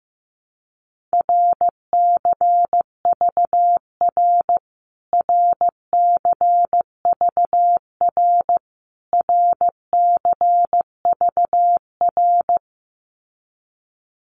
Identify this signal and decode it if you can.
Morse code, heard as a keyed tone, about 15 wpm, 700 Hz. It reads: RCVR RCVR RCVR
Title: Abbreviations Repeated 3x - 15wpm